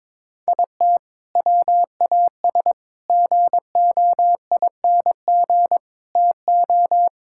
Morse code, heard as a keyed tone, about 22 wpm, 700 Hz. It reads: IT WAS GOING TO
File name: practice / 22wpm / Sentences with 4-Word Phrases - 22wpm